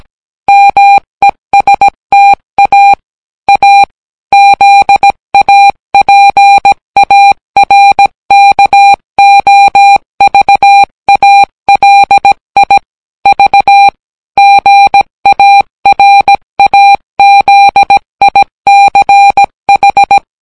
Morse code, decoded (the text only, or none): MESTA A ZAPARKOVALI V GARAZICH